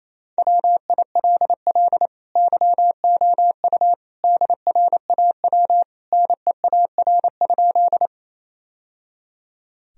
Morse code, heard as a keyed tone, about 28 wpm, 700 Hz. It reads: WILL YOU DRAW NEAR?
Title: Sentences from Top 1000 Words - Review Part 1 - 28wpm